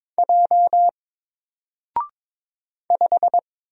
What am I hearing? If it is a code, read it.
Morse code, heard as a keyed tone, about 22 wpm, 700 Hz. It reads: J 5
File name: Single-Letter-Number-22wpm